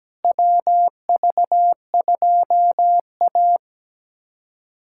Morse code, heard as a keyed tone, about 17 wpm, 700 Hz. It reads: WV2A
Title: USA Callsign DE Callsign - 4 Characters - 17wpm